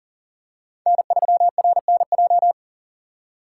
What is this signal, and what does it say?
Morse code, heard as a keyed tone, about 39 wpm, 700 Hz. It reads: N3RNJ
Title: USA Call Signs - 5 Characters - 3x Character Spacing - 40wpm